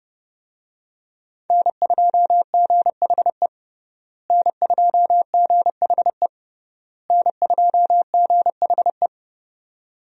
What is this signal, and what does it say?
Morse code, heard as a keyed tone, about 30 wpm, 700 Hz. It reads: N2GHE N2GHE N2GHE
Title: USA Call Signs - 5 Characters - Repeated 3x - 30wpm